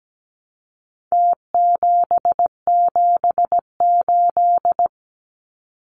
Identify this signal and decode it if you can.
Morse code, heard as a keyed tone, about 17 wpm, 700 Hz. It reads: T778
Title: Numbers - Contest Style - 4-Digit 17wpm